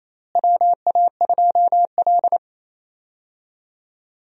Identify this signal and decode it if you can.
Morse code, heard as a keyed tone, about 28 wpm, 700 Hz. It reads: WA2L
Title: USA Call Signs - 4 Characters - Repeated 2x - 28wpm